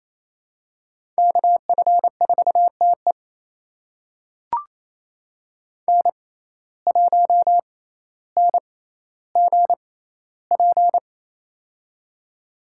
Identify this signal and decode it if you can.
Morse code, heard as a keyed tone, about 28 wpm, spaced as for 14 wpm, 700 Hz. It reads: KF4TE N1NGP
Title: USA Call Signs - 5 Characters - 6x Character Spacing - 28wpm